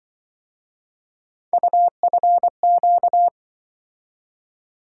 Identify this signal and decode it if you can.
Morse code, heard as a keyed tone, about 24 wpm, 700 Hz. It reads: UFQ
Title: Speed-Racing - Three Letter-Number Combinations - 20wpm